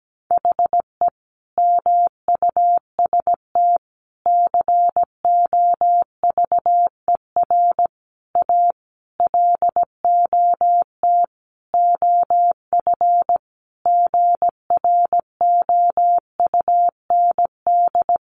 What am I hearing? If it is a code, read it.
Morse code, heard as a keyed tone, about 17 wpm, 700 Hz. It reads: HE MUST COVER A LOT OF GROUND
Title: Sentences from Top 500 Words 17wpm